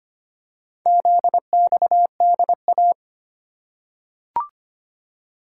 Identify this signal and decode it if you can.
Morse code, heard as a keyed tone, about 25 wpm, 700 Hz. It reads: ZXDA
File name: practice / 25wpm / Four Letter-Number Combinations - 25wpm